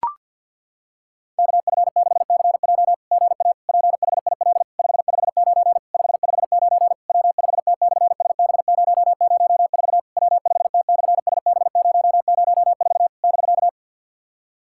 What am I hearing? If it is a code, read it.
Morse code, heard as a keyed tone, about 50 wpm, 700 Hz. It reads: KF6CJ GA PHIL 559 559 W5T/SB004 W5T/SB004 <BK>
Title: SOTA QSO Practice 50wpm